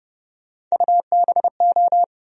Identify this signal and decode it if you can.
Morse code, heard as a keyed tone, about 31 wpm, 700 Hz. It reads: UBO
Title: USA Call Sign Postfixes - 3 Letters 30wpm